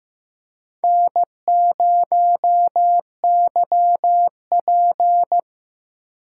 Morse code, heard as a keyed tone, about 15 wpm, 700 Hz. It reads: N0YP
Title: USA Call Signs - 4 Characters - 3x Character Spacing - 15wpm